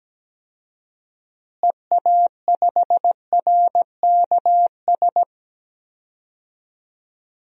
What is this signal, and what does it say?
Morse code, heard as a keyed tone, about 17 wpm, 700 Hz. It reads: EA5RKS
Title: DXCC-100 Call Signs - 6 Characters - Repeated 2x - 17wpm